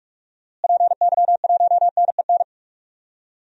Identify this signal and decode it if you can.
Morse code, heard as a keyed tone, about 46 wpm, 700 Hz. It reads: PY1NEN
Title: DXCC-100 Call Signs - 6 Characters - Repeated 3x - 45wpm